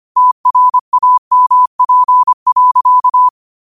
Morse code, heard as a keyed tone, about 25 wpm, 1000 Hz. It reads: TRAMP.
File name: qotdfiles / 25wpm